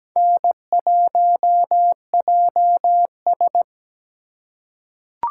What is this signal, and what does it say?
Morse code, heard as a keyed tone, about 17 wpm, 700 Hz. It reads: N1JS
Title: USA Call Signs - 4 Characters - 4x Character Spacing - 17wpm